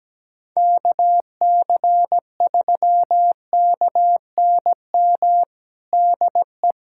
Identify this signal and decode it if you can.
Morse code, heard as a keyed tone, about 17 wpm, 700 Hz. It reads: KC3KNM DE
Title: USA Callsign DE Callsign - 6 Characters - 17wpm